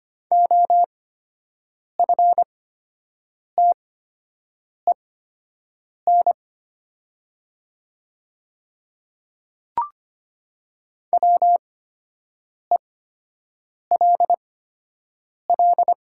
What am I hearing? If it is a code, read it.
Morse code, heard as a keyed tone, about 25 wpm, spaced as for 7 wpm, 700 Hz. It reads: OFTEN WELL
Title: 8x Character Spacing - Top 500 Words - 25wpm